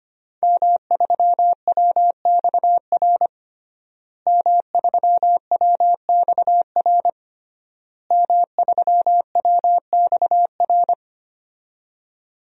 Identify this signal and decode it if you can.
Morse code, heard as a keyed tone, about 25 wpm, 700 Hz. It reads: M3WXR M3WXR M3WXR
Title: DXCC-100 Call Signs - 5 Characters - Repeated 3x - 25wpm